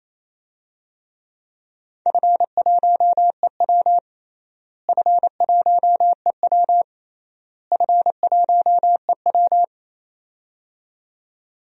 Morse code, heard as a keyed tone, about 27 wpm, 700 Hz. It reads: F1EW F1EW F1EW
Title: DXCC-100 Call Signs - 4 Characters - Repeated 3x - 28wpm